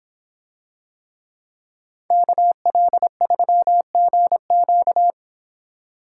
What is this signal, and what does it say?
Morse code, heard as a keyed tone, about 26 wpm, 700 Hz. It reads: KL3GQ